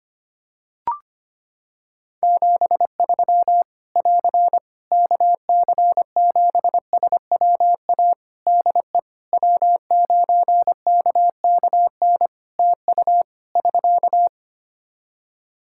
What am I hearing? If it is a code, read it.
Morse code, heard as a keyed tone, about 25 wpm, 700 Hz. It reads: 73 <AR> KC7SWA DE W9KKN TU <SK>
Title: Practice QSO - Exchange 4 - Ending - 25wpm